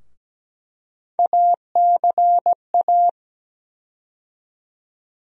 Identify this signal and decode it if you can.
Morse code, heard as a keyed tone, about 17 wpm, 700 Hz. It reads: ACA
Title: USA Call Sign Postfixes - 3 Letters 17wpm